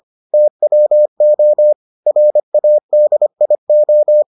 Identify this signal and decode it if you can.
Morse code, heard as a keyed tone, about 25 wpm, 600 Hz. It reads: TWO RADIO